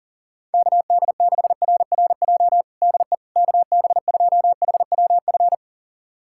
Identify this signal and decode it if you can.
Morse code, heard as a keyed tone, about 40 wpm, 700 Hz. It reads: KD6RRJ DE KB2HWF
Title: USA Callsign DE Callsign - 6 Characters - 40wpm